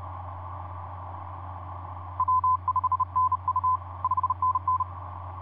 Morse code, heard as a keyed tone, about 31 wpm, 1020 Hz. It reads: W5NU 5NN
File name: vp8dmn ja1ghh etc 30m cw 23jan2010 2351